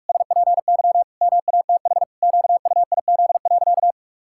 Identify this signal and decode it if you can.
Morse code, heard as a keyed tone, about 46 wpm, 700 Hz. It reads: SPY MATH QUIZ.
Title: Pangram Challenge v2 - 45wpm